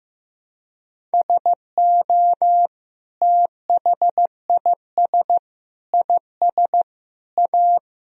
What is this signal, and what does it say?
Morse code, heard as a keyed tone, about 15 wpm, 700 Hz. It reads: SO THIS IS A
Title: Sentences from Top 1000 Words - Review Part 1 - 15wpm